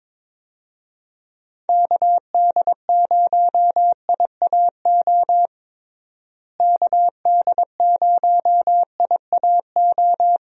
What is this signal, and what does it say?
Morse code, heard as a keyed tone, about 22 wpm, 700 Hz. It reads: KD0IAO KD0IAO
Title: USA Call Signs - 6 Characters - Repeated 3x - 22wpm